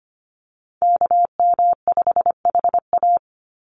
Morse code, heard as a keyed tone, about 25 wpm, 700 Hz. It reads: KM5HA
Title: USA Call Signs 25wpm